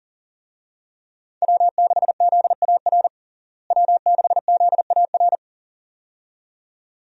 Morse code, heard as a keyed tone, about 37 wpm, 700 Hz. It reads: W6ZAR W6ZAR
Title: USA Call Signs - 5 Characters - Repeated 2x - 40wpm